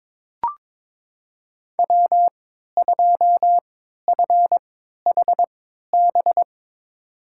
Morse code, heard as a keyed tone, about 22 wpm, spaced as for 12 wpm, 700 Hz. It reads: W2FHB
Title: USA Call Signs - 5 Characters - 3x Character Spacing - 22wpm